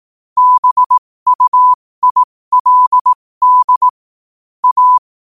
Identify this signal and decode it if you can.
Morse code, heard as a keyed tone, about 18 wpm, spaced as for 15 wpm, 1000 Hz. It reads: BUILD A
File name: qotdfiles / 15wpm